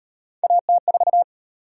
Morse code, heard as a keyed tone, about 33 wpm, 700 Hz. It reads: AT4